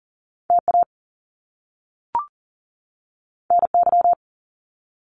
Morse code, heard as a keyed tone, about 39 wpm, 700 Hz. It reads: TA NY
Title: Top Two Letter Combinations in Words 40wpm